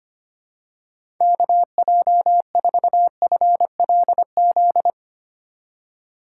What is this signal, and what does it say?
Morse code, heard as a keyed tone, about 25 wpm, 700 Hz. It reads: KJ4FLZ
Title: USA Call Signs - 6 Characters - 2x Character Spacing - 25wpm